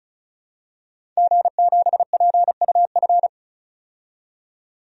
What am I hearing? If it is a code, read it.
Morse code, heard as a keyed tone, about 34 wpm, 700 Hz. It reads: G7PUF
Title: DXCC-100 Call Signs - 5 Characters - 35wpm